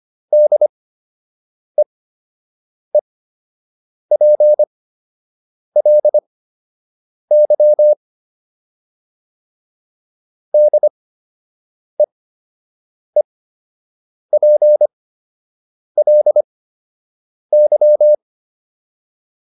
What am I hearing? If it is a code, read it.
Morse code, heard as a keyed tone, about 25 wpm, spaced as for 7 wpm, 600 Hz. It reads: DEEPLY DEEPLY